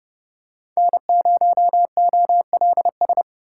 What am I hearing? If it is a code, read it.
Morse code, heard as a keyed tone, about 32 wpm, 700 Hz. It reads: N0OLS
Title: USA Call Signs - 5 Characters - 4x Character Spacing - 30wpm